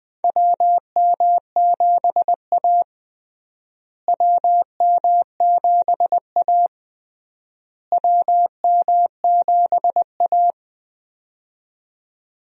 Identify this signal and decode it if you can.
Morse code, heard as a keyed tone, about 21 wpm, 700 Hz. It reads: WM7A WM7A WM7A
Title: USA Call Signs - 4 Characters - Repeated 3x - 20wpm